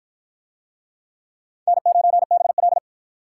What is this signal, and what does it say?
Morse code, heard as a keyed tone, about 51 wpm, 700 Hz. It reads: N9BL